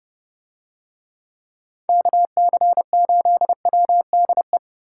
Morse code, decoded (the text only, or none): KC8WDE